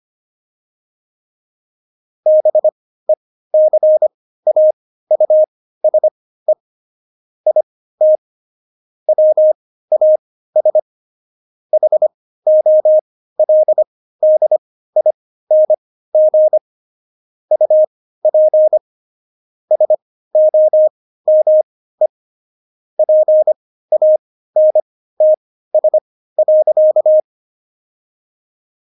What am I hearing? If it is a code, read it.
Morse code, heard as a keyed tone, about 25 wpm, spaced as for 15 wpm, 620 Hz. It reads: BECAUSE IT WAS HOLDING UP SOME PANTS.
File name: Jokes-25-char-15-eff